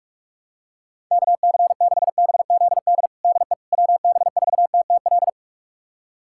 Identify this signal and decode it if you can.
Morse code, heard as a keyed tone, about 46 wpm, 700 Hz. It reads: KC6BZD DE WB4TTL
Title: USA Callsign DE Callsign - 6 Characters - 45wpm